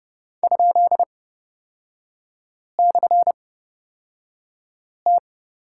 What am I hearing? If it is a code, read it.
Morse code, heard as a keyed tone, about 30 wpm, spaced as for 5 wpm, 700 Hz. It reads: ?/T
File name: Lesson 36e - BT - Three Characters 30wpm